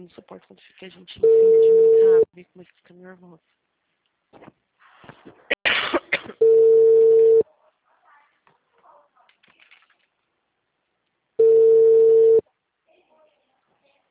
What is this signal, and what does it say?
Ringback tone (China, 450 Hz), with speech.